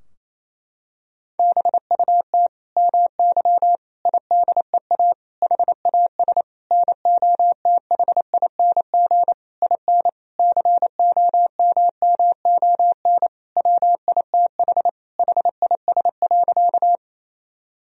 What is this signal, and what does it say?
Morse code, heard as a keyed tone, about 28 wpm, 700 Hz. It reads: BUT MY IDEA HAS NOTHING IN COMMON WITH HIS.
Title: Sentences from Top 1000 Words - Review Part 3 - 28wpm